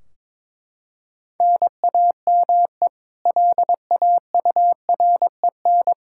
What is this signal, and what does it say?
Morse code, heard as a keyed tone, about 22 wpm, 700 Hz. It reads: NAME LAUREN
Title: Practice QSO - Exchange 1 - Beginning - 22wpm